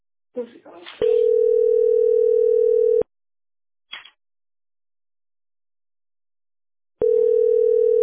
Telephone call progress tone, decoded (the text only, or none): ringback tone